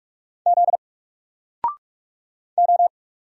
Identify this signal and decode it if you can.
Morse code, heard as a keyed tone, about 44 wpm, 700 Hz. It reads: Z O
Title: Single Character (Letter:Number:Punctuation:Prosign) 45wpm